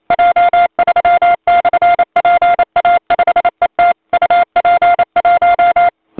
Morse code, heard as a keyed tone, about 28 wpm, 700 Hz. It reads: J3/PA5ET UP1
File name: voice / 80m j3 pa5et 002 2001 08 08